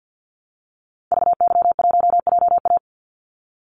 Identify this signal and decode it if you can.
Morse code, heard as a keyed tone, about 50 wpm, 700 Hz. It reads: VY1JA
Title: Call Signs Repeated 3x - 50wpm